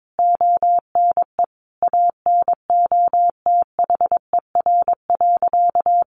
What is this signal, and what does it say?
Morse code, heard as a keyed tone, about 22 wpm, 700 Hz. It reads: ONE ANOTHER.
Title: Sentences from Top 1400 Words 22wpm